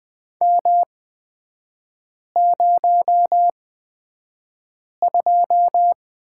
Morse code, heard as a keyed tone, about 20 wpm, spaced as for 5 wpm, 700 Hz. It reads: M02
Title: 224 Lesson 36e - BT - Three Characters 20wpm